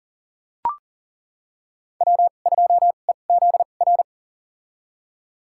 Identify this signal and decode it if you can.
Morse code, heard as a keyed tone, about 38 wpm, spaced as for 28 wpm, 700 Hz. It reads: W2EZR
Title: USA Call Signs - 5 Characters - 2x Character Spacing - 40wpm